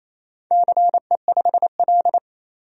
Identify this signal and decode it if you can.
Morse code, heard as a keyed tone, about 29 wpm, 700 Hz. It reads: CE5L
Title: DXCC-100 Call Signs - 4 Characters - 28wpm